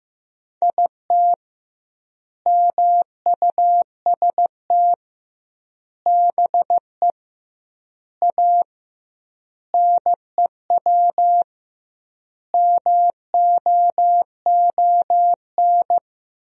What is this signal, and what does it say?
Morse code, heard as a keyed tone, about 15 wpm, 700 Hz. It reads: IT MUST BE A NEW MOON
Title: 2x Word-Spacing - Sentences from Top 500 Words - 15wpm